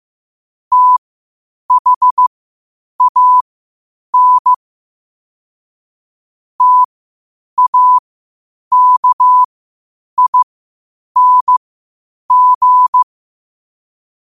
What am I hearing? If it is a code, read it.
Morse code, heard as a keyed tone, about 15 wpm, spaced as for 8 wpm, 1000 Hz. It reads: THAN TAKING
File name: qotdfiles / 8wpm